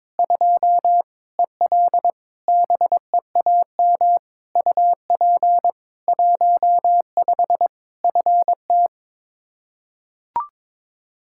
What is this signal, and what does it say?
Morse code, heard as a keyed tone, about 21 wpm, 700 Hz. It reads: 2 EL BEAM UP 15 FT
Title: Practice QSO - Exchange 2 - Info - 22wpm